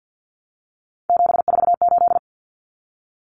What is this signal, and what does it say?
Morse code, heard as a keyed tone, about 50 wpm, 700 Hz. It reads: 748
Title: Common Cut Numbers - 3-Digit 50wpm